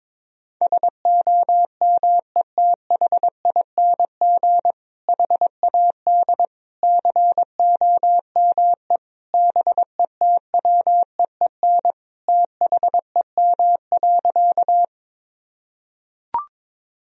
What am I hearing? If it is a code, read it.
Morse code, heard as a keyed tone, about 22 wpm, 700 Hz. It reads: SOMETHING HAD COME BETWEEN THEM.
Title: Sentences from Top 1000 Words - Review Part 1 - 22wpm